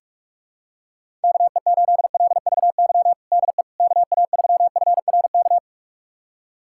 Morse code, heard as a keyed tone, about 46 wpm, 690 Hz. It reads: KE8LVY DE KA3FRK
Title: USA Callsign DE Callsign - 6 Characters - 45wpm